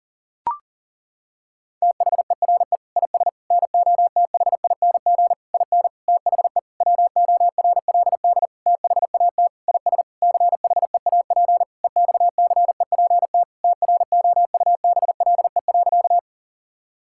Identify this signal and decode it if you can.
Morse code, heard as a keyed tone, about 40 wpm, 700 Hz. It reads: THERE IS NOTHING IN THE WORLD THAT IS CHEAP EXCEPT TROUBLE.
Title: Sentences from Top 1600 Words 40wpm